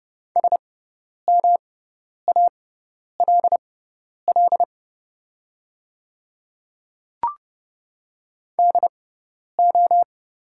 Morse code, heard as a keyed tone, about 30 wpm, spaced as for 10 wpm, 690 Hz. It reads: SMALL DO
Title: 6x Character Spacing - Top 200 Words - 30wpm